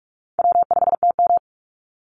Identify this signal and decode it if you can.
Morse code, heard as a keyed tone, about 44 wpm, 700 Hz. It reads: W5TM
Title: Easy Call Signs Repeated 3x - 45wpm